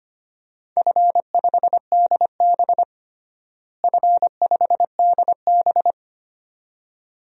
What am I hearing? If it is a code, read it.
Morse code, heard as a keyed tone, about 25 wpm, 700 Hz. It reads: F5DB F5DB
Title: DXCC-100 Call Signs - 4 Characters - Repeated 2x - 25wpm